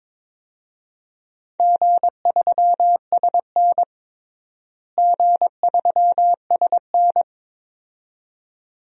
Morse code, heard as a keyed tone, about 22 wpm, 700 Hz. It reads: G3SN G3SN
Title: DXCC-100 Call Signs - 4 Characters - Repeated 2x - 22wpm